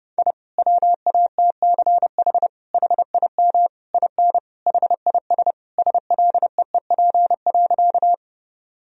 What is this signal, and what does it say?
Morse code, heard as a keyed tone, about 30 wpm, 700 Hz. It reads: I WATCH HIM IN HIS SLEEP.
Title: Sentences from Top 1000 Words - Review Part 1 - 30wpm